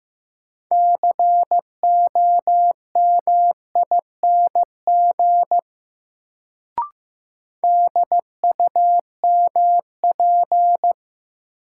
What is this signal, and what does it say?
Morse code, heard as a keyed tone, about 15 wpm, 700 Hz. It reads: COMING DUMP
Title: Sentences from Top 1300 Words - New Word Drill - 15wpm